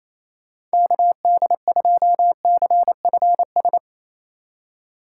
Morse code, heard as a keyed tone, about 35 wpm, 700 Hz. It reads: KD2CFS
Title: Speed Racing - USA 2x3 Call Signs 28wpm